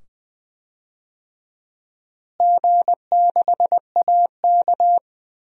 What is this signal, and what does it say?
Morse code, heard as a keyed tone, about 21 wpm, 700 Hz. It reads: G6AK
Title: DXCC-100 Call Signs - 4 Characters - Repeated 2x - 20wpm